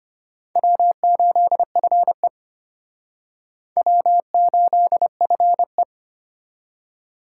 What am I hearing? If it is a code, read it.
Morse code, heard as a keyed tone, about 31 wpm, 700 Hz. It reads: W8FE W8FE
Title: Speed Racing - USA Call Signs 25wpm